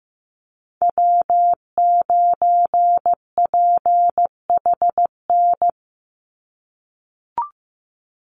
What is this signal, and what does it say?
Morse code, heard as a keyed tone, about 15 wpm, 700 Hz. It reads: W9PHN
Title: USA Call Signs 15wpm